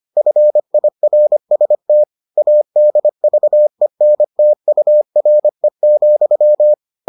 Morse code, heard as a keyed tone, about 25 wpm, 600 Hz. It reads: FIRST ADVENTURE,